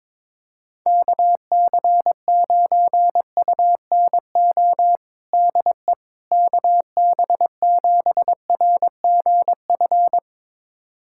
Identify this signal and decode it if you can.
Morse code, heard as a keyed tone, about 22 wpm, 700 Hz. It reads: KC9UNO DE KB7RGF
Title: USA Callsign DE Callsign 22wpm